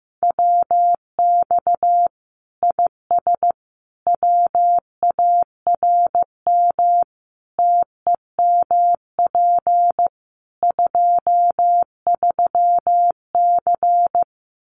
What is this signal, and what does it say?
Morse code, heard as a keyed tone, about 15 wpm, 700 Hz. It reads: WX IS WARM TEMP 23C
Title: Common QSO Phrases 15wpm